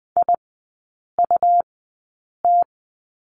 Morse code, heard as a keyed tone, about 20 wpm, spaced as for 8 wpm, 700 Hz. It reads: IUT
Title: 107 Lesson 18e - Letter m - Three Characters 20wpm